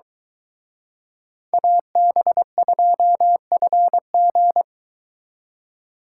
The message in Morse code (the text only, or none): AB2FG